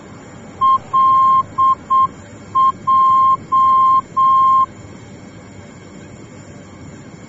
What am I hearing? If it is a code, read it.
Morse code, heard as a keyed tone, about 7 wpm, 1060 Hz. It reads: LJ
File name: germany / mx / LJ-D